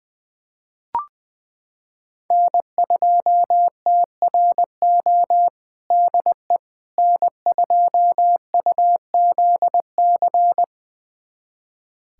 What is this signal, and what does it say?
Morse code, heard as a keyed tone, about 21 wpm, 700 Hz. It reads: N2TRO DE N2UZC
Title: USA Callsign DE Callsign - 5 Characters - 20wpm